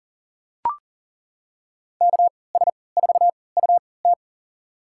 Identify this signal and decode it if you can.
Morse code, heard as a keyed tone, about 40 wpm, spaced as for 23 wpm, 700 Hz. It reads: KS4UT
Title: USA Call Signs - 5 Characters - 3x Character Spacing - 40wpm